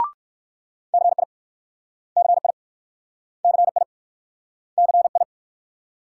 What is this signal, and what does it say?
Morse code, heard as a keyed tone, about 56 wpm, 700 Hz. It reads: CI CI CI CI
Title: Speed-Racing Multi-letter Phonemes - 45wpm